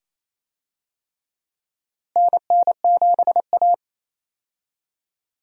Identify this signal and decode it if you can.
Morse code, heard as a keyed tone, about 28 wpm, 700 Hz. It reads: NN7A
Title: USA Call Signs - 4 Characters - Repeated 2x - 28wpm